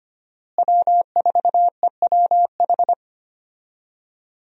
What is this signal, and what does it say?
Morse code, heard as a keyed tone, about 25 wpm, 700 Hz. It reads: W4EWH
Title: USA Call Signs - 5 Characters - Repeated 3x - 25wpm